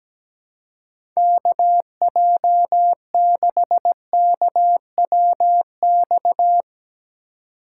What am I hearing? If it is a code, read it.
Morse code, heard as a keyed tone, about 17 wpm, 700 Hz. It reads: KJ6KWX
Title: USA Call Signs - 6 Characters - 3x Character Spacing - 17wpm